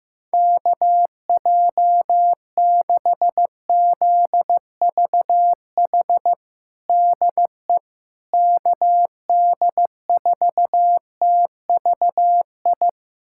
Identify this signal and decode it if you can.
Morse code, heard as a keyed tone, about 15 wpm, 700 Hz. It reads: KJ6ZVH DE KD4TVI
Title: USA Callsign DE Callsign 15wpm